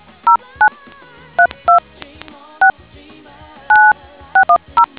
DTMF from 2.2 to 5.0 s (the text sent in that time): 6 9610